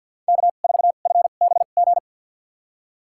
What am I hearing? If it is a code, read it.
Morse code, heard as a keyed tone, about 49 wpm, spaced as for 36 wpm, 700 Hz. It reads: K4FBG
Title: USA Call Signs - 5 Characters - 2x Character Spacing - 50wpm